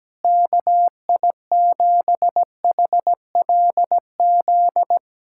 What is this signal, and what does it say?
Morse code, heard as a keyed tone, about 17 wpm, 700 Hz. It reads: KI7HLZ
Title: USA Callsign DE Callsign - 6 Characters - 17wpm